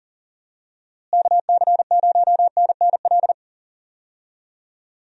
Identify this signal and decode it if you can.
Morse code, heard as a keyed tone, about 40 wpm, 700 Hz. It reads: KC0NNL
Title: USA Call Signs - 6 Characters - Repeated 3x - 40wpm